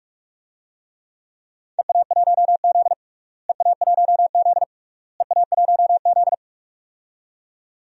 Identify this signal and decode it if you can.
Morse code, heard as a keyed tone, about 42 wpm, 700 Hz. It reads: EA1Z EA1Z EA1Z
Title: DXCC-100 Call Signs - 4 Characters - Repeated 3x - 45wpm